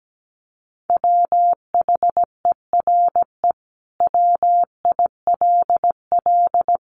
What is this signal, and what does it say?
Morse code, heard as a keyed tone, about 17 wpm, 700 Hz. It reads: WHERE WILL
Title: Sentences from Top 900 Words 17wpm